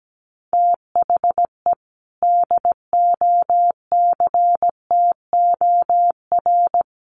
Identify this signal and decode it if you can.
Morse code, heard as a keyed tone, about 17 wpm, 700 Hz. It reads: THE DOCTOR
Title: Sentences from Top 1500 Words 17wpm